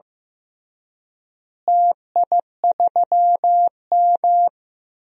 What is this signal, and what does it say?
Morse code, heard as a keyed tone, about 15 wpm, 700 Hz. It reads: TI3M
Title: DXCC-100 Call Signs - 4 Characters - Repeated 2x - 15wpm